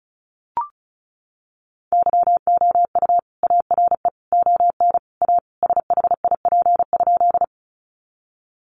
Morse code, heard as a keyed tone, about 34 wpm, 700 Hz. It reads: YOU ARE ON A SHIP?
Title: Sentences from Top 400 Words 35wpm